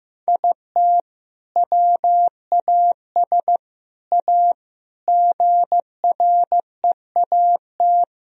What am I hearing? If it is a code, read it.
Morse code, heard as a keyed tone, about 15 wpm, 700 Hz. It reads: IT WAS A GREAT
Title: Sentences from Top 1000 Words - Review Part 2 - 15wpm